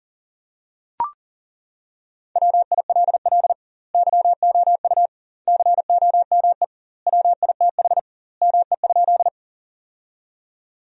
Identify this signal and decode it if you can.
Morse code, heard as a keyed tone, about 40 wpm, 700 Hz. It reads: WILL YOU COME WITH ME?
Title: 2x Word-Spacing - Sentences from Top 200 Words - 40wpm